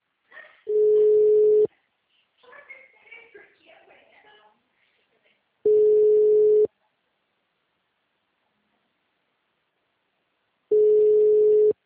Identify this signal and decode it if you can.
Ringback tone (Europe, 425 Hz), with speech.